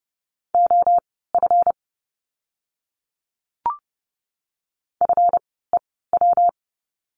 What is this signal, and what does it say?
Morse code, heard as a keyed tone, about 30 wpm, spaced as for 17 wpm, 700 Hz. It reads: OF FEW
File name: course / 30:10wpm / 237 Lesson 38f - Letter q - Words 30wpm